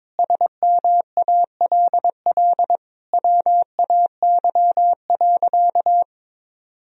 Morse code, heard as a keyed tone, about 22 wpm, 700 Hz. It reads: SMALL WAY.
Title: Sentences from Top 1000 Words - Review Part 1 - 22wpm